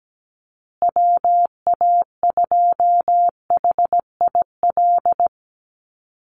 Morse code, heard as a keyed tone, about 17 wpm, 700 Hz. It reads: WA2HIL
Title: USA Call Signs Repeated 3x 17wpm